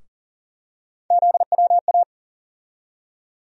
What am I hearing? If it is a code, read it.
Morse code, heard as a keyed tone, about 40 wpm, 700 Hz. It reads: ZWA
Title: USA Call Sign Postfixes - 3 Letters 40wpm